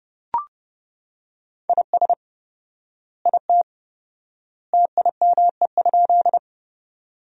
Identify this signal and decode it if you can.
Morse code, heard as a keyed tone, about 30 wpm, 700 Hz. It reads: IS IT TIME?
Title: 4x Word-Spacing - Sentences from Top 100 Words - 30wpm